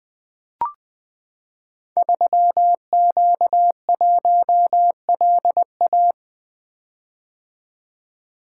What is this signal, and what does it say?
Morse code, heard as a keyed tone, about 20 wpm, 700 Hz. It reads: 3Q1LA
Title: Code Groups - 5 Letter:Numbers - 20wpm